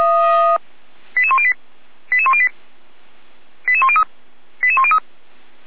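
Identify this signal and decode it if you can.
Five-tone selective call (ZVEI), heard as quick runs of five tones, tones of about 70 ms. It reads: 88108 88108 88103 88103
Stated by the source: ZVEI five-tone selective call received by SDR, Orlamünde fire brigade (feuerwehr-orlamuende).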